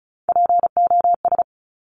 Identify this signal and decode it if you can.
Morse code, heard as a keyed tone, about 36 wpm, 700 Hz. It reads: POS
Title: Top Three Letter Combinations in Words 35wpm